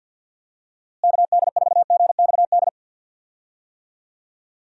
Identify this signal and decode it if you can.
Morse code, heard as a keyed tone, about 51 wpm, 700 Hz. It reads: KD4GXD
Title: USA Call Signs - 6 Characters - Repeated 2x - 50wpm